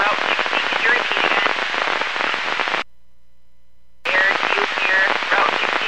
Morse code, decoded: none

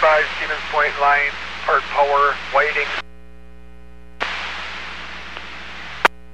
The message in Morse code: none

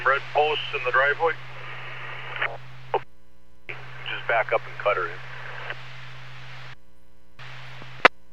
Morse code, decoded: none

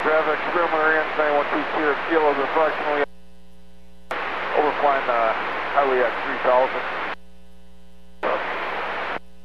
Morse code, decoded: none

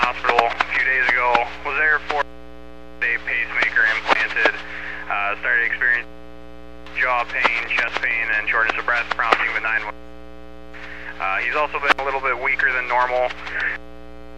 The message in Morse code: none

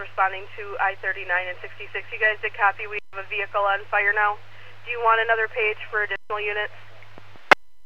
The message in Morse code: none

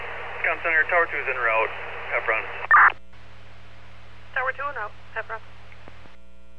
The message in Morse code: none

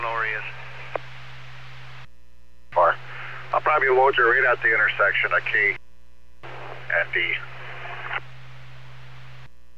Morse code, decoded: none

